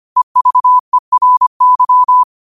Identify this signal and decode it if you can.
Morse code, heard as a keyed tone, about 25 wpm, 1000 Hz. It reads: EVERY
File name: qotdfiles / 25wpm